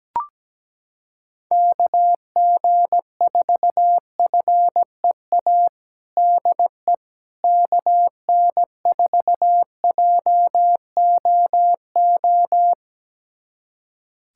Morse code, heard as a keyed tone, about 17 wpm, 700 Hz. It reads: KG4FEA DE KN4JOO
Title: USA Callsign DE Callsign - 6 Characters - 17wpm